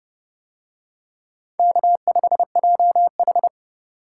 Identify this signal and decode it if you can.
Morse code, heard as a keyed tone, about 25 wpm, 700 Hz. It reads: K5JH
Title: Speed Racing - USA 1x2 Call Signs 20wpm